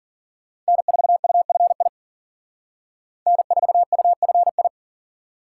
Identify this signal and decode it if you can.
Morse code, heard as a keyed tone, about 51 wpm, 700 Hz. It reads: N4UFI N4UFI